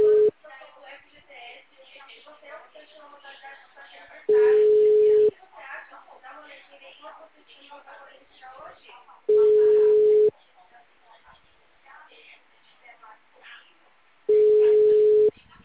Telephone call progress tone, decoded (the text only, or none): ringback tone